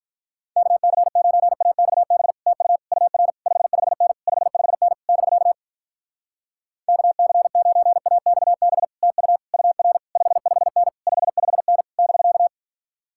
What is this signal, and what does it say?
Morse code, heard as a keyed tone, about 48 wpm, 700 Hz. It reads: KC9AXB TU UR 55N 55N <BK> KC9AXB TU UR 55N 55N <BK>